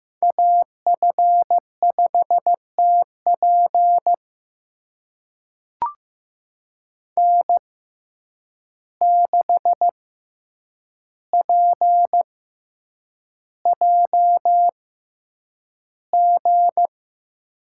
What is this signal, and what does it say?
Morse code, heard as a keyed tone, about 15 wpm, spaced as for 8 wpm, 700 Hz. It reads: AF5TP N6PJG
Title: USA Call Signs - 5 Characters - 6x Character Spacing - 15wpm